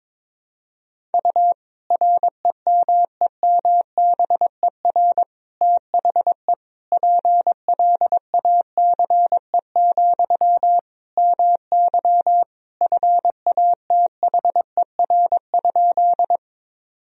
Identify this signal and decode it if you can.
Morse code, heard as a keyed tone, about 22 wpm, 700 Hz. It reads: U REMEMBER THE PLACE, MY FATHER?
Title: Sentences from Top 400 Words and Abbreviations 22wpm